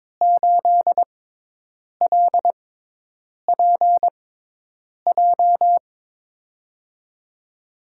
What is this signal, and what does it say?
Morse code, heard as a keyed tone, about 23 wpm, spaced as for 8 wpm, 700 Hz. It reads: 8LPJ
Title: USA Call Signs - 6 Characters - 6x Character Spacing - 22wpm